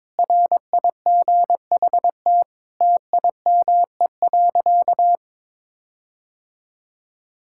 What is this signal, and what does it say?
Morse code, heard as a keyed tone, about 22 wpm, 700 Hz. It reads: RIGHT TIME.
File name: Sentences with 9-Word Phrases - 22wpm